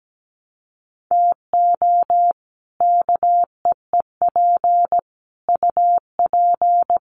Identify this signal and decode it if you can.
Morse code, heard as a keyed tone, about 17 wpm, 700 Hz. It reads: TO KEEP UP
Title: Sets of 5 Words Encore 17wpm